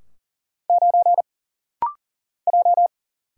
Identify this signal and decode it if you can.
Morse code, heard as a keyed tone, about 40 wpm, 700 Hz. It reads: 9 J
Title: Single Letter-Number - Rapid-Fire 40wpm